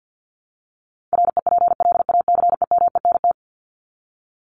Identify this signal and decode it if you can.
Morse code, heard as a keyed tone, about 50 wpm, 700 Hz. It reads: REPLACEMENT
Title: Top 1751-2000 Words 50wpm